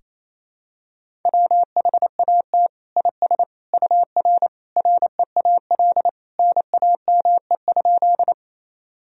Morse code, heard as a keyed tone, about 28 wpm, 700 Hz. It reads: WHAT IS UR REAL NAME?
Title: Sentences from Top 300 Words and Abbreviations 28wpm